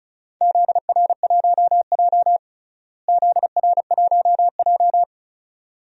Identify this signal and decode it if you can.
Morse code, heard as a keyed tone, about 36 wpm, 700 Hz. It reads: ZR1J ZR1J
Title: DXCC-100 Call Signs - 4 Characters - Repeated 2x - 35wpm